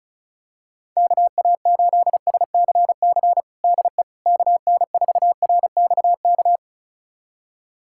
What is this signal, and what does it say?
Morse code, heard as a keyed tone, about 34 wpm, 700 Hz. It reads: KA8SCC DE KN4RXK
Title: USA Callsign DE Callsign 35wpm